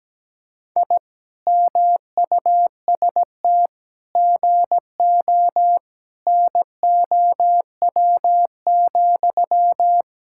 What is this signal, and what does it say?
Morse code, heard as a keyed tone, about 17 wpm, 700 Hz. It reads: I MUST GO NOW,
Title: Sentences from Top 1000 Words - Review Part 5 - 17wpm